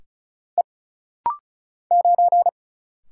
Morse code, heard as a keyed tone, about 36 wpm, 700 Hz. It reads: E 9
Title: Single Letter-Number Alt - Rapid-Fire 35wpm